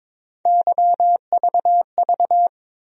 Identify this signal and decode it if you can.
Morse code, heard as a keyed tone, about 22 wpm, 700 Hz. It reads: YVV
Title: USA Call Sign Postfixes - 3 Letters 22wpm